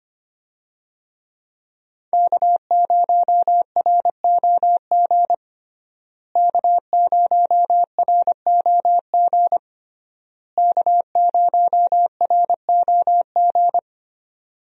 Morse code, heard as a keyed tone, about 25 wpm, 700 Hz. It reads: K0ROG K0ROG K0ROG
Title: USA Call Signs - 5 Characters - Repeated 3x - 25wpm